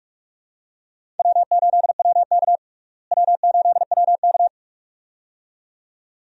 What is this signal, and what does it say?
Morse code, heard as a keyed tone, about 46 wpm, 700 Hz. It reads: W8WK W8WK
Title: USA Call Signs - 4 Characters - Repeated 2x - 45wpm